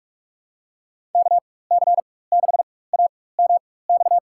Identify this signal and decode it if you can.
Morse code, heard as a keyed tone, about 45 wpm, spaced as for 21 wpm, 700 Hz. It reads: KC6AMX
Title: USA Call Signs - 4x Character Spacing - 45wpm